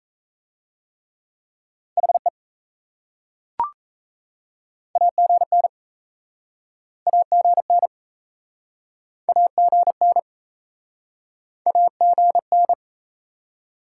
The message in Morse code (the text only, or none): HE AGN AGN AGN AGN